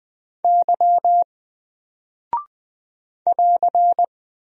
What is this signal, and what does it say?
Morse code, heard as a keyed tone, about 20 wpm, 700 Hz. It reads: Y <AR>
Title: Single Character (Letter:Number:Punctuation:Prosign) Alt 20wpm